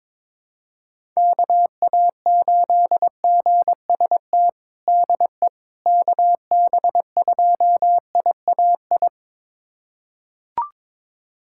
Morse code, heard as a keyed tone, about 21 wpm, 700 Hz. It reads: KA8GST DE KB2IAI
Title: USA Callsign DE Callsign - 6 Characters - 22wpm